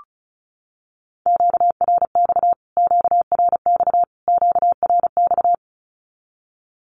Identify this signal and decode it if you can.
Morse code, heard as a keyed tone, about 34 wpm, 700 Hz. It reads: QRX QRX QRX
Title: Q-Codes Repeated 3x 35wpm